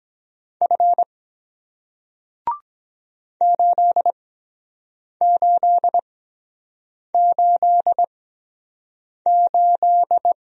Morse code, heard as a keyed tone, about 23 wpm, spaced as for 7 wpm, 700 Hz. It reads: F 8888